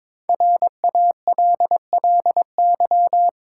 Morse code, heard as a keyed tone, about 22 wpm, 690 Hz. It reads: RALLY
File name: 5 Letter Words - First 3 Letters Given - 22wpm